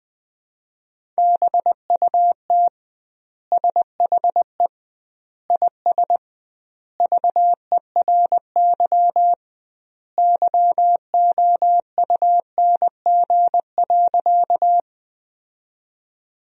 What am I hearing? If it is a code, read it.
Morse code, heard as a keyed tone, about 20 wpm, 700 Hz. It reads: BUT SHE IS VERY YOUNG.
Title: 2x Word-Spacing - Sentences from Top 400 Words - 20wpm